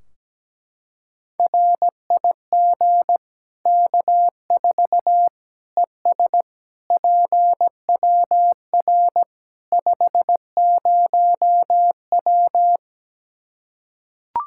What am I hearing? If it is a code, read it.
Morse code, heard as a keyed tone, about 17 wpm, 700 Hz. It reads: RIG K4 ES PWR 50W
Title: Practice QSO - Exchange 2 - Info - 17wpm